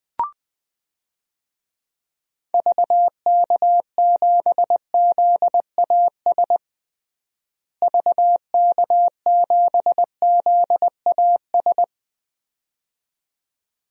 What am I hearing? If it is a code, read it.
Morse code, heard as a keyed tone, about 20 wpm, 700 Hz. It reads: VK7ZAS VK7ZAS
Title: DXCC-100 Call Signs - 6 Characters - Repeated 2x - 20wpm